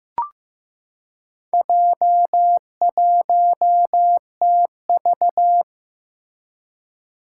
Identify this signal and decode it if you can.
Morse code, heard as a keyed tone, about 15 wpm, 700 Hz. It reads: J1TV
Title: Four Challenging Letter-Number Combinations 15wpm